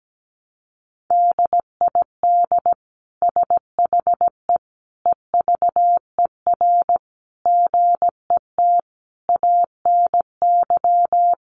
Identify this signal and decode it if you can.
Morse code, heard as a keyed tone, about 17 wpm, 700 Hz. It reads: DID SHE EVER GET ANY
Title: Sentences from Top 1100 Words 17wpm